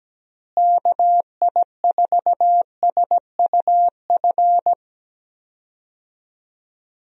Morse code, heard as a keyed tone, about 17 wpm, 700 Hz. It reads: KI4SUF
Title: USA Call Signs - 6 Characters - Repeated 2x - 17wpm